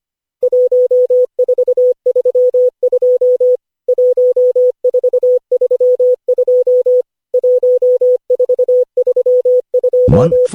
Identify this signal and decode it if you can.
Morse code, heard as a keyed tone, about 25 wpm, 500 Hz. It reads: 1432 1432 1432
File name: cw / Session5 cw training